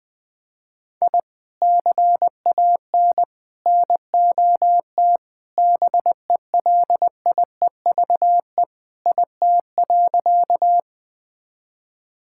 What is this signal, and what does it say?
Morse code, heard as a keyed tone, about 20 wpm, 690 Hz. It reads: I CAN NOT BELIEVE IT.
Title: Sentences with 4-Word Phrases - Spoken Beginning - 20wpm